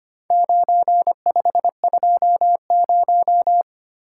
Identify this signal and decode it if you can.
Morse code, heard as a keyed tone, about 25 wpm, 700 Hz. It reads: 9520
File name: practice / 25wpm / Four Letter-Number Combinations - 25wpm